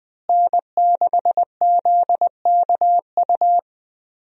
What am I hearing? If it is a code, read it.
Morse code, heard as a keyed tone, about 20 wpm, 700 Hz. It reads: N6ZKU
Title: USA Call Signs - 5 Characters - Repeated 2x - 20wpm